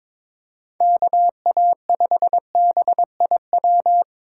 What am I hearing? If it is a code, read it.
Morse code, heard as a keyed tone, about 22 wpm, 700 Hz. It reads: KA5BIW
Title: USA Call Signs - 6 Characters - 2x Character Spacing - 22wpm